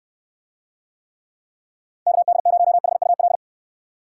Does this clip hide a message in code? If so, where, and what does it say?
Yes: Morse code from 2.1 to 3.4 s, heard as a keyed tone, about 62 wpm, 700 Hz. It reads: KD0HUL